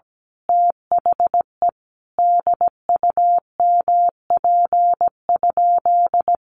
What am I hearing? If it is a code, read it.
Morse code, heard as a keyed tone, about 17 wpm, 700 Hz. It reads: THE DUMP?
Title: Sentences from Top 1300 Words 17wpm